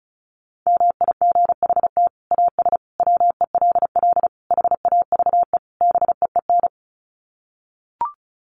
Morse code, heard as a keyed tone, about 34 wpm, 700 Hz. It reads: MIGHT AS WELL HAVE BEEN
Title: Sets of 5 Words Encore2 35wpm